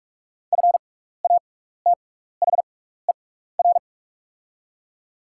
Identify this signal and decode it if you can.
Morse code, heard as a keyed tone, about 44 wpm, spaced as for 15 wpm, 700 Hz. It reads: FATHER
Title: 6x Character Spacing - Top 200 Words - 45wpm